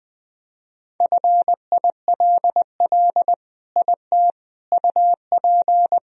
Morse code, heard as a keyed tone, about 20 wpm, 700 Hz. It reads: FILL IT UP
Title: Sentences from Top 1000 Words - Review Part 3 - 20wpm